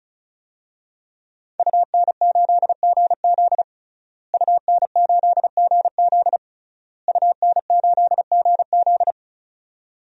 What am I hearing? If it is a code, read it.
Morse code, heard as a keyed tone, about 36 wpm, 700 Hz. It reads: UN8GZ UN8GZ UN8GZ
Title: DXCC-100 Call Signs - 5 Characters - Repeated 3x - 35wpm